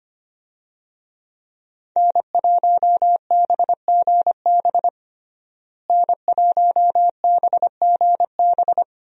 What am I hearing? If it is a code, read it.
Morse code, heard as a keyed tone, about 25 wpm, 700 Hz. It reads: N1BGB N1BGB
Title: USA Call Signs - 5 Characters - Repeated 2x - 25wpm